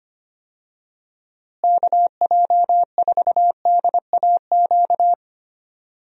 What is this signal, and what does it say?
Morse code, heard as a keyed tone, about 25 wpm, 700 Hz. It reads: KJ4DAQ
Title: USA Call Signs - 6 Characters - Repeated 3x - 25wpm